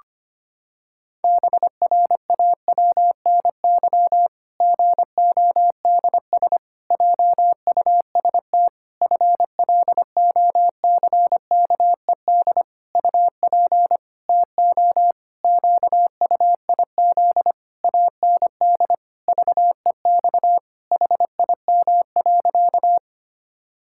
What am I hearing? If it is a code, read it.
Morse code, heard as a keyed tone, about 25 wpm, 700 Hz. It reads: BRAWNY GODS JUST FLOCKED UP TO QUIZ AND VEX HIM.
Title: Pangram Challenge v2 - 25wpm